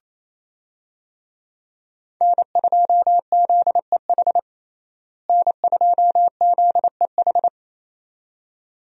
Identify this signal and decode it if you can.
Morse code, heard as a keyed tone, about 28 wpm, 700 Hz. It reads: N2ZEH N2ZEH
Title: USA Call Signs - 5 Characters - Repeated 2x - 28wpm